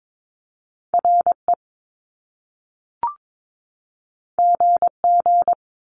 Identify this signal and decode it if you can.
Morse code, heard as a keyed tone, about 22 wpm, 700 Hz. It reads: RE GG
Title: Top Two Letter Combinations in Words 22wpm